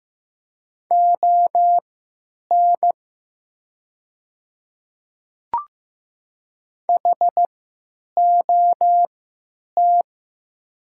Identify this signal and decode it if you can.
Morse code, heard as a keyed tone, about 15 wpm, spaced as for 9 wpm, 690 Hz. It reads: ON HOT
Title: 3x Character Spacing - Top 100 Words - 15wpm